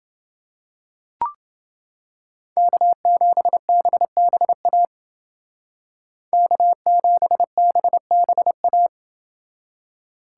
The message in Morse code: K7BBA K7BBA

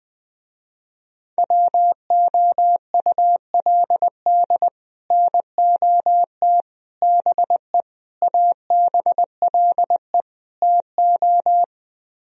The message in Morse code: WOULD NOT BE ABLE TO

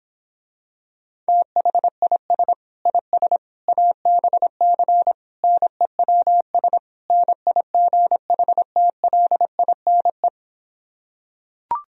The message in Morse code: THIS IS ABC NEWS NIGHTLINE

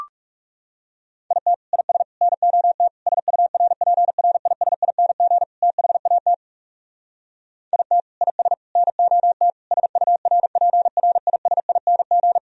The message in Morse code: IT IS NOT SURPRISING THAT IT IS NOT SURPRISING

